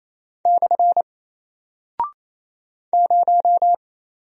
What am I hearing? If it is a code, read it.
Morse code, heard as a keyed tone, about 28 wpm, 700 Hz. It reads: / 0
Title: Single Character (Letter:Number:Punctuation:Prosign) Alt 28wpm